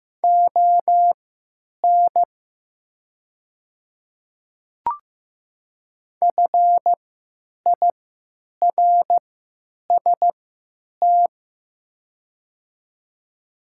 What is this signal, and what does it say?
Morse code, heard as a keyed tone, about 15 wpm, spaced as for 9 wpm, 700 Hz. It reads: ON FIRST
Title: 3x Character Spacing - Top 200 Words - 15wpm